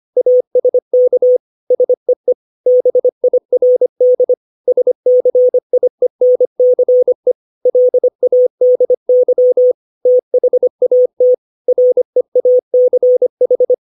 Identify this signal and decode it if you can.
Morse code, heard as a keyed tone, about 25 wpm, 500 Hz. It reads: ASK SEE BIRD SCIENCE LADY THAT REACH
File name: 632 words / 632 Words 25WPM